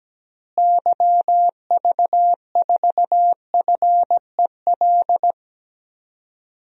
Morse code, heard as a keyed tone, about 17 wpm, 700 Hz. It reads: YV4FEL
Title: DXCC-100 Call Signs - 6 Characters - Repeated 3x - 17wpm